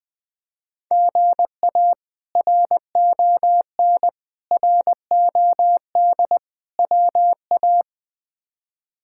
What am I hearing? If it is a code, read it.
Morse code, heard as a keyed tone, about 20 wpm, 700 Hz. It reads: GA RON ROD WA
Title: K1USN SST QSO Practice - 20wpm